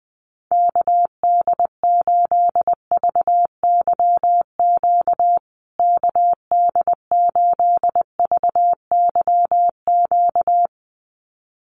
Morse code, heard as a keyed tone, about 20 wpm, 700 Hz. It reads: KD8VYQ KD8VYQ
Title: USA Call Signs Repeated 2x 20wpm